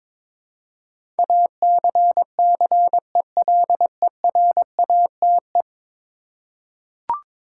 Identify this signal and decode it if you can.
Morse code, heard as a keyed tone, about 21 wpm, 700 Hz. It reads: ACCELERATE
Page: morse-code-ninja-practice-22wpm